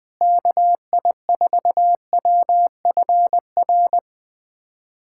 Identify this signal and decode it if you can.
Morse code, heard as a keyed tone, about 21 wpm, 700 Hz. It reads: KI4WFR
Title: USA Callsign DE Callsign 20wpm